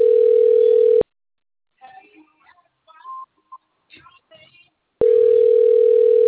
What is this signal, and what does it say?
Ringback tone (North America), 440 + 480 Hz.